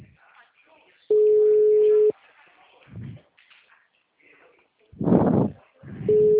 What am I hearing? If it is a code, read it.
Ringback tone (Europe, 425 Hz), with speech.
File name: out- -2040- - - .2238